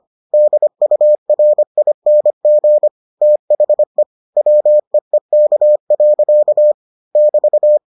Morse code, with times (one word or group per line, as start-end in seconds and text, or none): 0.3-2.9 DURING
3.2-4.0 THE
4.4-6.7 WEEK.
7.2-7.8 <BT>